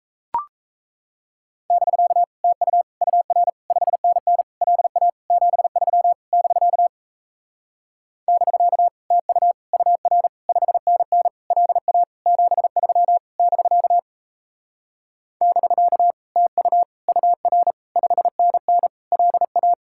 Morse code, heard as a keyed tone, about 37 wpm, 700 Hz. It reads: <BK> TU UR 5NN LA 73 <BK> <BK> TU UR 5NN LA 73 <BK> <BK> TU UR 5NN LA